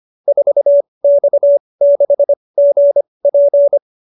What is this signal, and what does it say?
Morse code, heard as a keyed tone, about 25 wpm, spaced as for 20 wpm, 590 Hz. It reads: 4X6GP
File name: QSO310 20